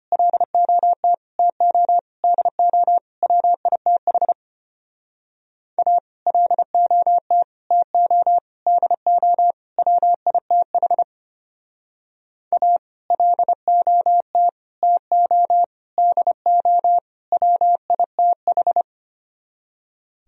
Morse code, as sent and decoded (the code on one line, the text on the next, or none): .-.. --- - / - --- / -.. --- / .-- .. - .... / .- / .-.. --- - / - --- / -.. --- / .-- .. - .... / .- / .-.. --- - / - --- / -.. --- / .-- .. - ....
LOT TO DO WITH A LOT TO DO WITH A LOT TO DO WITH